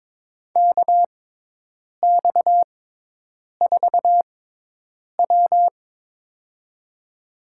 Morse code, heard as a keyed tone, about 22 wpm, spaced as for 8 wpm, 700 Hz. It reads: KX4W
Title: USA Call Signs - 4 Characters - 6x Character Spacing - 22wpm